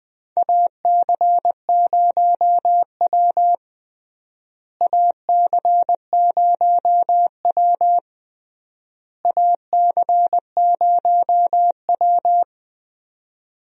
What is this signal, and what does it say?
Morse code, heard as a keyed tone, about 20 wpm, 700 Hz. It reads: AC0W AC0W AC0W
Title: USA Call Signs - 4 Characters - Repeated 3x - 20wpm